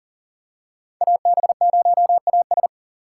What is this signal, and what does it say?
Morse code, heard as a keyed tone, about 40 wpm, 700 Hz. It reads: AB0AS
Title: USA Call Signs - 5 Characters - 3x Character Spacing - 40wpm